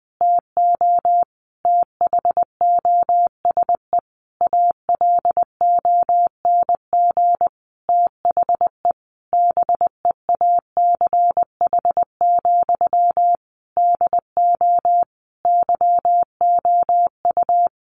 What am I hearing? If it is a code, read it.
Morse code, heard as a keyed tone, about 20 wpm, 700 Hz. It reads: TO THOSE ALONG THE BEACH, DO YOU
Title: Sentences from Top 1500 Words 20wpm